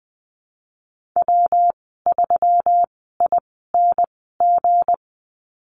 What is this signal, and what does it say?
Morse code, heard as a keyed tone, about 19 wpm, spaced as for 14 wpm, 700 Hz. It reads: W3ING
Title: 193 Lesson 31g - Letter v - Call Signs 20wpm